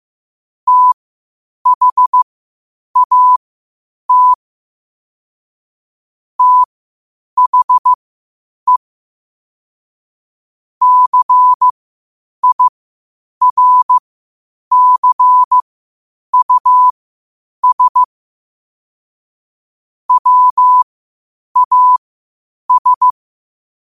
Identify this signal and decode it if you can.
Morse code, heard as a keyed tone, about 15 wpm, spaced as for 8 wpm, 1000 Hz. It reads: THAT THE CIRCUS WAS
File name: qotdfiles / 8wpm